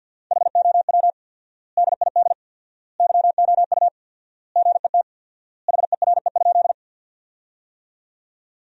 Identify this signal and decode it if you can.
Morse code, heard as a keyed tone, about 48 wpm, 700 Hz. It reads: HOW DID YOU GET HERE?
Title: 4x Word-Spacing - Sentences from Top 200 Words - 50wpm